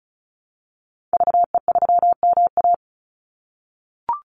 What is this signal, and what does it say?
Morse code, heard as a keyed tone, about 34 wpm, 700 Hz. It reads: VE3MA
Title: Call Signs - 35wpm